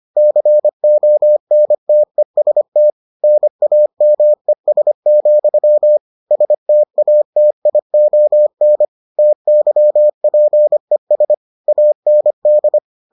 Morse code, heard as a keyed tone, about 25 wpm, 600 Hz. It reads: CONTEST NAMES, STATION TYPES AND